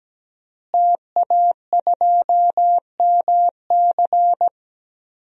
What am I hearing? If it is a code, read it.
Morse code, heard as a keyed tone, about 17 wpm, 700 Hz. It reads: TA2MC
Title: DXCC-100 Call Signs - 5 Characters - Repeated 3x - 17wpm